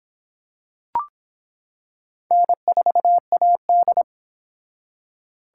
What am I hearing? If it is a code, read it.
Morse code, heard as a keyed tone, about 24 wpm, 700 Hz. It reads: N4AD